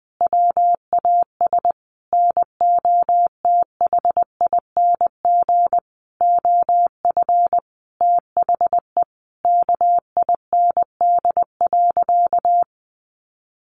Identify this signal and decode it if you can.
Morse code, heard as a keyed tone, about 19 wpm, 700 Hz. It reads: WAS NOTHING OF THE KIND.
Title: Sentences from Top 1200 Words 20wpm